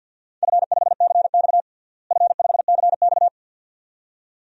Morse code, heard as a keyed tone, about 53 wpm, 700 Hz. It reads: F5CX F5CX
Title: DXCC-100 Call Signs - 4 Characters - Repeated 2x - 50wpm